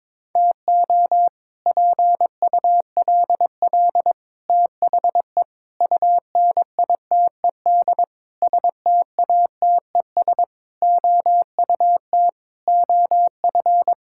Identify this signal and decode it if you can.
Morse code, heard as a keyed tone, about 22 wpm, 700 Hz. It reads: TO PULL THE UNITED STATES OUT OF
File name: practice / 22wpm / Sets of 8 Words - 22wpm